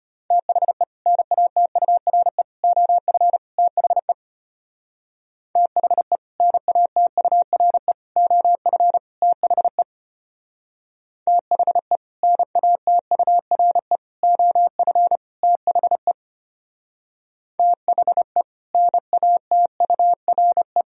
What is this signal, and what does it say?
Morse code, heard as a keyed tone, about 32 wpm, 700 Hz. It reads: THE NATURE OF THE THE NATURE OF THE THE NATURE OF THE THE NATURE